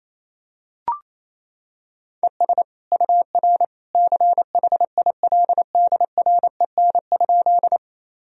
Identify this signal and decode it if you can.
Morse code, heard as a keyed tone, about 28 wpm, 700 Hz. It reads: ES UR CHILDREN?
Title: Sentences from Top 300 Words and Abbreviations 28wpm